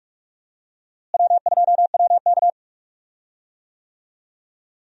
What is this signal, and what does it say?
Morse code, heard as a keyed tone, about 41 wpm, 700 Hz. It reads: W2WK
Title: USA Call Signs - 4 Characters - Repeated 3x - 45wpm